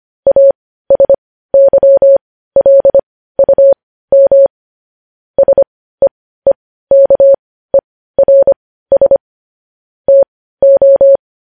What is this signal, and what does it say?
Morse code, heard as a keyed tone, about 25 wpm, spaced as for 15 wpm, 550 Hz. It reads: ASYLUM SEEKERS TO
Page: news headlines in morse code at 15 wpm